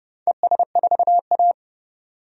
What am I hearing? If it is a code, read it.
Morse code, heard as a keyed tone, about 30 wpm, 700 Hz. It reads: ES4A
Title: DXCC-100 Call Signs - 4 Characters - 30wpm